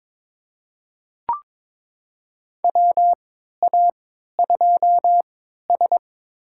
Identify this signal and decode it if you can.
Morse code, heard as a keyed tone, about 22 wpm, spaced as for 13 wpm, 700 Hz. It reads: WA2S
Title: USA Call Signs - 4 Characters - 3x Character Spacing - 22wpm